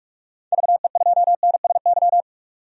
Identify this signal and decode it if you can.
Morse code, heard as a keyed tone, about 46 wpm, 700 Hz. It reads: VE2NSY
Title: DXCC-100 Call Signs - 6 Characters - 45wpm